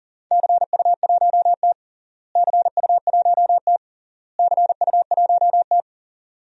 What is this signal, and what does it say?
Morse code, heard as a keyed tone, about 40 wpm, 700 Hz. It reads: CU1T CU1T CU1T
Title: DXCC-100 Call Signs - 4 Characters - Repeated 3x - 40wpm